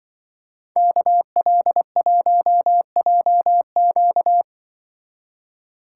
Morse code, heard as a keyed tone, about 24 wpm, 700 Hz. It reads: KL1JQ
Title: Speed Racing - USA 2x2 Call Signs 20wpm